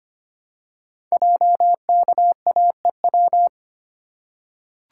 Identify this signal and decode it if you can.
Morse code, heard as a keyed tone, about 25 wpm, 700 Hz. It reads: JKAEW
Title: Code Groups - 5 Letters - 25wpm